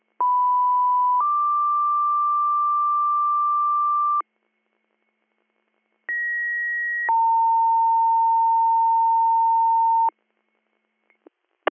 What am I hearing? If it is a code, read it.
Two-tone page, heard as a one-second tone, then a three-second tone, about 990 then 1180 Hz, about 1900 then 910 Hz. It is heard from 0.2 to 10.1 s.